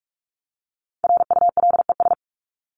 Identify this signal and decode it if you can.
Morse code, heard as a keyed tone, about 44 wpm, 700 Hz. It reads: RULES
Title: ES:S Suffix - Top 100 Words with Base Words 45wpm